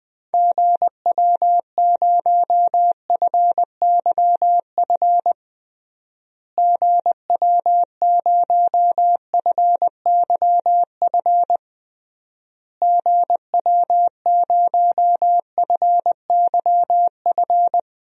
Morse code, heard as a keyed tone, about 21 wpm, 700 Hz. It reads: GW0FYF GW0FYF GW0FYF
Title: DXCC-100 Call Signs - 6 Characters - Repeated 3x - 20wpm